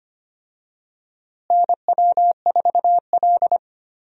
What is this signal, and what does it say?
Morse code, heard as a keyed tone, about 25 wpm, 700 Hz. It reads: NW4L
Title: USA Call Signs - 4 Characters - Repeated 3x - 25wpm